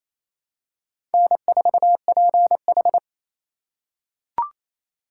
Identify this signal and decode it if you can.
Morse code, heard as a keyed tone, about 28 wpm, 700 Hz. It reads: N4PH
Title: USA Call Signs - 4 Characters - 3x Character Spacing - 28wpm